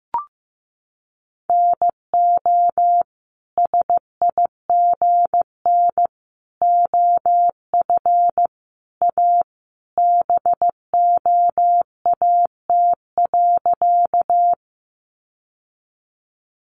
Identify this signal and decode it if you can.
Morse code, heard as a keyed tone, about 15 wpm, 700 Hz. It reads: NO SIGN OF A BOAT.
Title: Sentences from Top 1000 Words 15wpm